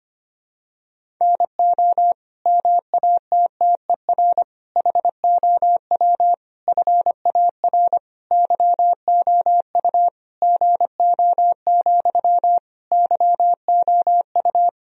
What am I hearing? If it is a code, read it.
Morse code, heard as a keyed tone, about 25 wpm, 700 Hz. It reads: NO MATTER HOW FAR YOU GO, YOU
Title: Sentences from Top 1000 Words - Review Part 1 - 25wpm